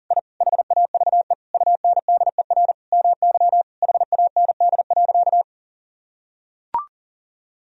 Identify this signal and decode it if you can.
Morse code, heard as a keyed tone, about 40 wpm, 700 Hz. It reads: I HAVE UNDER MY HAND.
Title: Sentences from Top 1000 Words - Review Part 1 - 40wpm